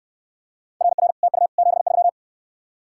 Morse code, heard as a keyed tone, about 69 wpm, 700 Hz. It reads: RR TU 73